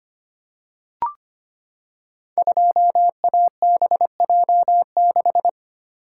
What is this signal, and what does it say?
Morse code, heard as a keyed tone, about 25 wpm, 700 Hz. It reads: 2ABJ6
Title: Code Groups - 5 Letter:Numbers - 25wpm